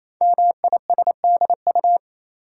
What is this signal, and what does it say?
Morse code, heard as a keyed tone, about 27 wpm, 700 Hz. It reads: MISDU
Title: Code Groups - 5 Letters - 28wpm